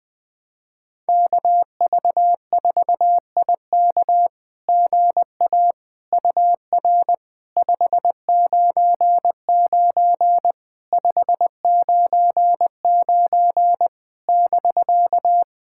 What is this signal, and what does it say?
Morse code, heard as a keyed tone, about 20 wpm, 700 Hz. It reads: KV4IK GA UR 599 599 <BK>
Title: SOTA QSO Practice 20wpm